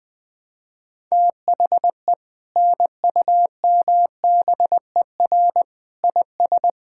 Morse code, heard as a keyed tone, about 21 wpm, 700 Hz. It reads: THE NUMBER IS
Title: Sentences with 2-Word Phrases - Spoken Beginning - 20wpm